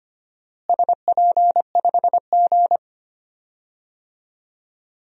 Morse code, heard as a keyed tone, about 25 wpm, 700 Hz. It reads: SP5G
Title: DXCC-100 Call Signs - 4 Characters - Repeated 3x - 25wpm